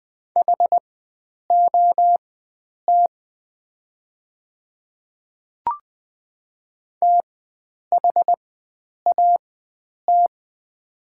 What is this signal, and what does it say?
Morse code, heard as a keyed tone, about 20 wpm, spaced as for 9 wpm, 700 Hz. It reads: HOT THAT
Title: 4x Character Spacing - Top 100 Words - 20wpm